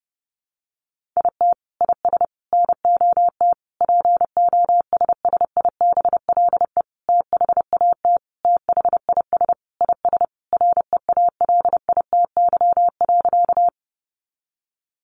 Morse code, heard as a keyed tone, about 30 wpm, 700 Hz. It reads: IT IS NOT POSSIBLE THAT THIS IS REALITY.
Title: Sentences from Top 1200 Words 30wpm